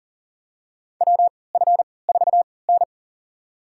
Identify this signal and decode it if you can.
Morse code, heard as a keyed tone, about 40 wpm, spaced as for 23 wpm, 700 Hz. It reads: WF4N
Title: USA Call Signs - 4 Characters - 3x Character Spacing - 40wpm